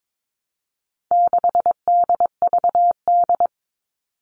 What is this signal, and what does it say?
Morse code, heard as a keyed tone, about 22 wpm, 700 Hz. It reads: 6DVD
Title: All Cut Numbers - 4-Digit 22wpm